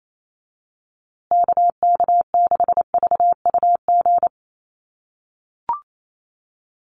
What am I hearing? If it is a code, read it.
Morse code, heard as a keyed tone, about 28 wpm, 700 Hz. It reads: KK6VUG
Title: USA Call Signs Repeated 3x 28wpm